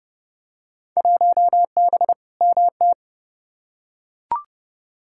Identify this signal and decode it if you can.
Morse code, heard as a keyed tone, about 30 wpm, 700 Hz. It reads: 1B MT
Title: Field Day Exchanges - 30wpm